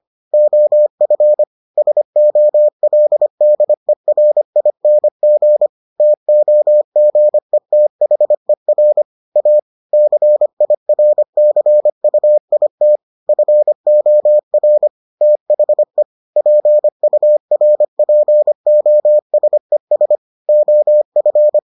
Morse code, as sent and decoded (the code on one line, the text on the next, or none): --- ..-. / ... --- .-.. -.. .-. .. -. --. / - --- --. - .... .-. / .- / -.-. .. .-. -.-. ..- .. - / ..-. --- .-. / - .... / .--. ..- .-. .--. --- ... ... / --- ..-.
OF SOLDERING TOGETHER A CIRCUIT FOR THE PURPOSES OF